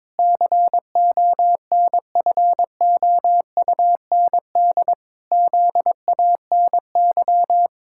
Morse code, heard as a keyed tone, about 22 wpm, 700 Hz. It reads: CONFOUND ZANY
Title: Pangram Challenge v2 - 22wpm